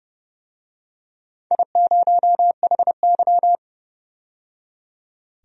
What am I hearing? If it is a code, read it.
Morse code, heard as a keyed tone, about 30 wpm, 700 Hz. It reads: I0HY
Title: DXCC-100 Call Signs - 4 Characters - 30wpm